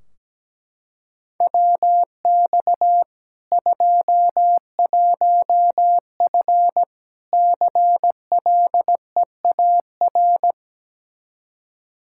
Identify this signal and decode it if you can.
Morse code, heard as a keyed tone, about 17 wpm, 700 Hz. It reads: WX 21F CLEAR
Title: Practice QSO - Exchange 2 - Info - 17wpm